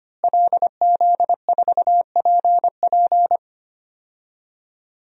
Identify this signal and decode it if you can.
Morse code, heard as a keyed tone, about 25 wpm, 700 Hz. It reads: LZ4PP
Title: DXCC-100 Call Signs - 5 Characters - Repeated 3x - 25wpm